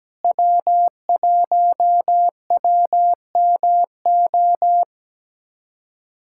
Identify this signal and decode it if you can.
Morse code, heard as a keyed tone, about 17 wpm, 700 Hz. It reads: W1WMO
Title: USA Callsign DE Callsign - 5 Characters - 17wpm